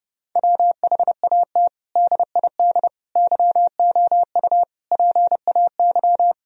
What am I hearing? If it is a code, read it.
Morse code, heard as a keyed tone, about 32 wpm, 700 Hz. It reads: WHAT DID YOU PAY
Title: Sentences from Top 1000 Words - Review Part 3 - 30wpm